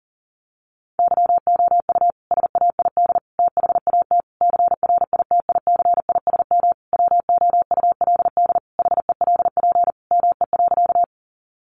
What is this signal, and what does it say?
Morse code, heard as a keyed tone, about 40 wpm, 700 Hz. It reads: YOU SAID THAT CRITICISM WOULD HELP ME.
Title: Sentences from Top 1300 Words 40wpm